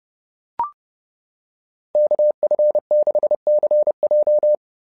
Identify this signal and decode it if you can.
Morse code, heard as a keyed tone, about 30 wpm, 600 Hz. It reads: KF6CJ
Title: SOTA QSO Practice 30wpm